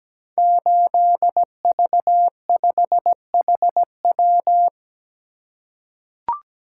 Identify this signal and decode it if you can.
Morse code, heard as a keyed tone, about 17 wpm, 700 Hz. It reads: 8V5HW
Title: Code Groups - 5 Letter:Numbers - 17wpm